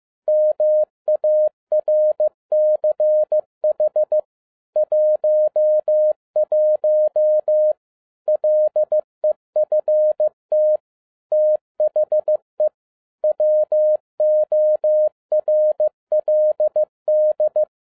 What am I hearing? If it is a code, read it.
Morse code, heard as a keyed tone, about 15 wpm, 610 Hz. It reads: MARCH 11 LEFT THE WORLD
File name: 2011-10-mp3 / n02-FIELD DAY CONTEST QRV FROM SAKAE VILLAGE 15wpm